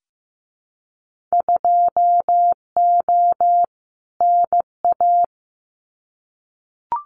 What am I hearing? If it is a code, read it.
Morse code, heard as a keyed tone, about 15 wpm, 700 Hz. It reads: 2O NA
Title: Winter Field Day Exchanges 15wpm